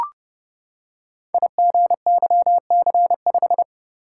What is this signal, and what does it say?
Morse code, heard as a keyed tone, about 30 wpm, 700 Hz. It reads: IGYC5
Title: Code Groups - 5 Letter:Numbers - 30wpm